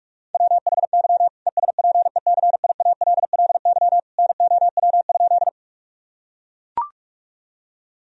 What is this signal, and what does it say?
Morse code, heard as a keyed tone, about 44 wpm, 700 Hz. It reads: WHY ESPECIALLY NOW?
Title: Sentences from Top 1000 Words - Review Part 4 - 45wpm